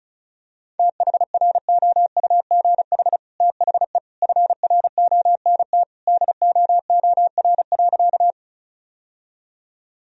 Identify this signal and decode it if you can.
Morse code, heard as a keyed tone, about 34 wpm, 700 Hz. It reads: THROUGH THE FRONT DOOR.
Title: Sentences from Top 1000 Words - Review Part 2 - 35wpm